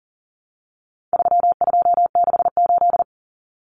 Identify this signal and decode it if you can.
Morse code, heard as a keyed tone, about 40 wpm, 700 Hz. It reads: 3268
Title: Numbers - 4-Digit 40wpm